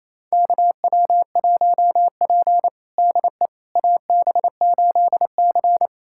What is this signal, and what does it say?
Morse code, heard as a keyed tone, about 27 wpm, 700 Hz. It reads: KW1P DE AB8C
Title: USA Callsign DE Callsign - 4 Characters - 28wpm